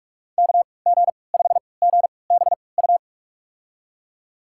Morse code, heard as a keyed tone, about 46 wpm, spaced as for 26 wpm, 700 Hz. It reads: KG5GBU
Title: USA Call Signs - 6 Characters - 3x Character Spacing - 45wpm